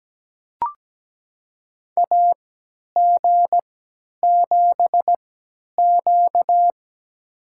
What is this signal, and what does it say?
Morse code, heard as a keyed tone, about 17 wpm, spaced as for 10 wpm, 700 Hz. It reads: AG7Q
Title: USA Call Signs - 3x Character Spacing - 17wpm